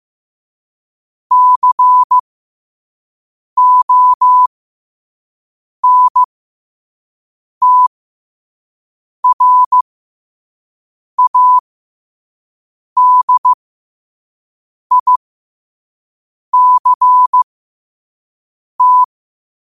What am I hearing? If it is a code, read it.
Morse code, heard as a keyed tone, about 15 wpm, spaced as for 5 wpm, 1000 Hz. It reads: CONTRADICT
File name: qotdfiles / 5wpm